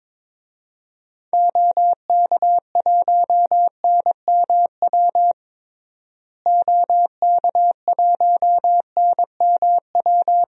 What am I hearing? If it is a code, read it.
Morse code, heard as a keyed tone, about 22 wpm, 700 Hz. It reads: OK1NMW OK1NMW
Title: DXCC-100 Call Signs - 6 Characters - Repeated 3x - 22wpm